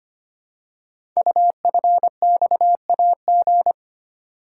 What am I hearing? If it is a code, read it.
Morse code, heard as a keyed tone, about 25 wpm, 700 Hz. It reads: UFXAG
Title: Code Groups - 5 Letters - 25wpm